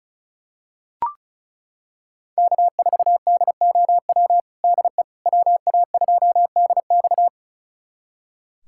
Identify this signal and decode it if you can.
Morse code, heard as a keyed tone, about 36 wpm, 700 Hz. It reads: K4DOW DE WA2DX
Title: USA Callsign DE Callsign - 5 Characters - 35wpm